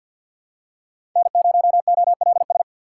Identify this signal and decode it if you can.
Morse code, heard as a keyed tone, about 51 wpm, 700 Hz. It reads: N0OLS
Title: USA Call Signs - 5 Characters - 2x Character Spacing - 50wpm